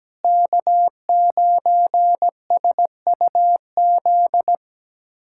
Morse code, heard as a keyed tone, about 17 wpm, 700 Hz. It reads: K9SUZ
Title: USA Callsign DE Callsign - 5 Characters - 17wpm